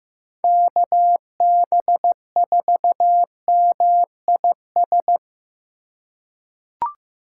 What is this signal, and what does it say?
Morse code, heard as a keyed tone, about 15 wpm, 700 Hz. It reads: KB4MIS
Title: USA Call Signs - 6 Characters - 3x Character Spacing - 15wpm